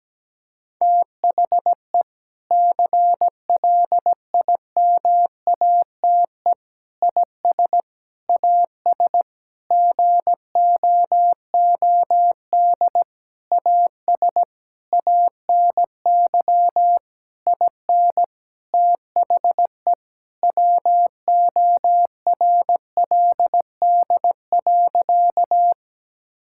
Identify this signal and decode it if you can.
Morse code, heard as a keyed tone, about 17 wpm, 700 Hz. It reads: THE CLIMATE IS AS GOOD AS ANY IN THE WORLD.
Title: Sentences from Top 1600 Words 17wpm